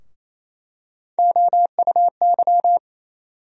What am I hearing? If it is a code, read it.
Morse code, heard as a keyed tone, about 28 wpm, 700 Hz. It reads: OUY
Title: USA Call Sign Postfixes - 3 Letters 28wpm